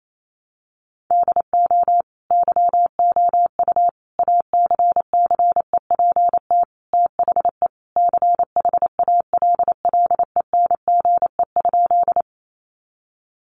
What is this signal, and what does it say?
Morse code, heard as a keyed tone, about 28 wpm, 700 Hz. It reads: DO YOU ACCEPT THE CHALLENGE?
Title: Sentences from Top 1300 Words 28wpm